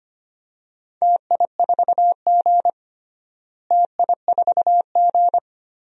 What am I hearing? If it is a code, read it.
Morse code, heard as a keyed tone, about 25 wpm, 700 Hz. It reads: TI4G TI4G
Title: DXCC-100 Call Signs - 4 Characters - Repeated 2x - 25wpm